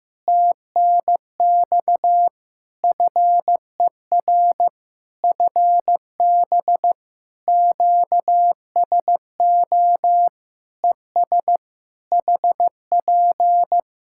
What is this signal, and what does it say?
Morse code, heard as a keyed tone, about 15 wpm, 700 Hz. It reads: TNX FER FB QSO ES HP
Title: Practice QSO - Exchange 4 - Ending - 15wpm